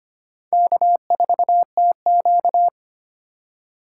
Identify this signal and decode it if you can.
Morse code, heard as a keyed tone, about 34 wpm, spaced as for 30 wpm, 700 Hz. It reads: K4TQ
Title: Speed Racing - USA 1x2 Call Signs 25wpm